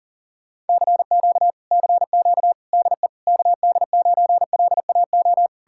Morse code, heard as a keyed tone, about 52 wpm, 700 Hz. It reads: CQ CQ DE KD9LAO